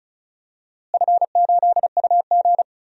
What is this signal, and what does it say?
Morse code, heard as a keyed tone, about 36 wpm, 690 Hz. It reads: F8UG
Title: DXCC-100 Call Signs - 4 Characters - Repeated 2x - 35wpm